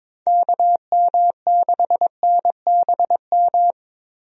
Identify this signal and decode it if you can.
Morse code, heard as a keyed tone, about 22 wpm, 700 Hz. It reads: KM6NBM
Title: USA Call Signs - 6 Characters - Repeated 2x - 22wpm